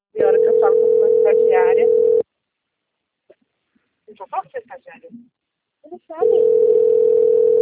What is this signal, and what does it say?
Ringback tone (North America), 440 + 480 Hz, with speech.
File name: out- - - - - .16363